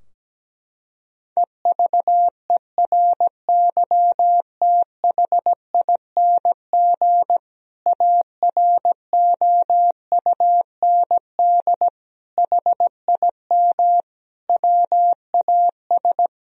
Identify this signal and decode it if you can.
Morse code, heard as a keyed tone, about 17 wpm, 700 Hz. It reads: EVERYTHING AROUND HIM WAS
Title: Sentences from Top 1000 Words - Review Part 3 - 17wpm